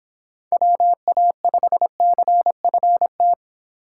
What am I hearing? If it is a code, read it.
Morse code, heard as a keyed tone, about 26 wpm, 700 Hz. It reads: WA5CFT